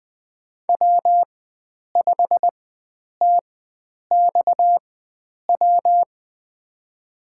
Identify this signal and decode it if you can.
Morse code, heard as a keyed tone, about 20 wpm, spaced as for 9 wpm, 700 Hz. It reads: W5TXW
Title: USA Call Signs - 5 Characters - 4x Character Spacing - 20wpm